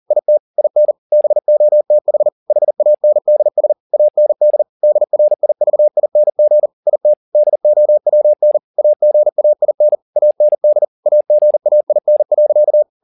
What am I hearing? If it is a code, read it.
Morse code, heard as a keyed tone, about 40 wpm, 600 Hz. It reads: IT IN BOTH HANDS AND DRIVING IT DOWN AGAIN AND AGAIN.
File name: the old man and the sea 2 40wpm